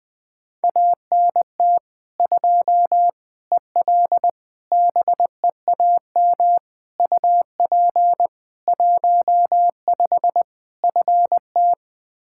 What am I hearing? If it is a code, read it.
Morse code, heard as a keyed tone, about 20 wpm, 700 Hz. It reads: ANT 2 EL BEAM UP 15 FT
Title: Practice QSO - Exchange 2 - Info - 20wpm